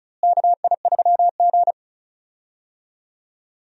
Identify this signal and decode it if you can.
Morse code, heard as a keyed tone, about 36 wpm, 700 Hz. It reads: KI3G
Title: USA Call Signs - 4 Characters - Repeated 3x - 35wpm